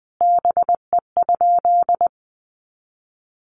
Morse code, heard as a keyed tone, about 20 wpm, 700 Hz. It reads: BE?
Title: Sentences from Top 100 Words 20wpm